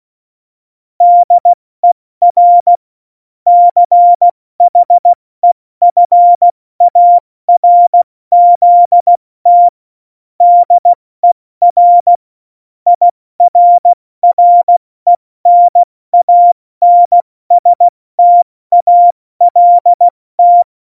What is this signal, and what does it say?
Morse code, heard as a keyed tone, about 16 wpm, spaced as for 14 wpm, 700 Hz. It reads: DER CHEFARZT DER IRRENANSTALT
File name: Morsekurs / Witze / Witz 27. Februar 2018, 70 F80